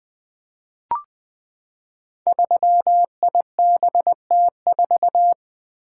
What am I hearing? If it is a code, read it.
Morse code, heard as a keyed tone, about 20 wpm, 700 Hz. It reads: 3IBT4
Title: Code Groups - 5 Letter:Numbers - 20wpm